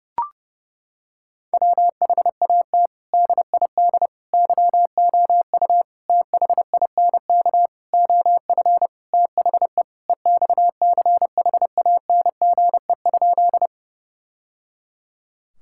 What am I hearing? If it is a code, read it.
Morse code, heard as a keyed tone, about 30 wpm, 700 Hz. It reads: WHAT DID YOU THINK OF THE EXCHANGE?
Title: Sentences from Top 1000 Words - Review Part 6 - 30wpm